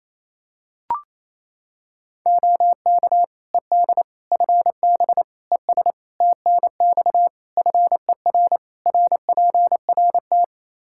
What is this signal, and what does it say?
Morse code, heard as a keyed tone, about 27 wpm, 700 Hz. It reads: OK ED FB ES TNX FER RPRT
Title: K1USN SST QSO Practice 28wpm